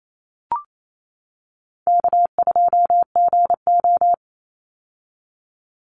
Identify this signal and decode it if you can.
Morse code, heard as a keyed tone, about 28 wpm, 700 Hz. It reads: K2GO
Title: Easy Call Signs - 28wpm